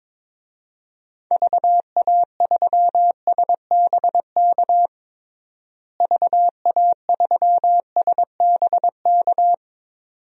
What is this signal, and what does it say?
Morse code, heard as a keyed tone, about 22 wpm, 700 Hz. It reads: VA3SBK VA3SBK
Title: DXCC-100 Call Signs - 6 Characters - Repeated 2x - 22wpm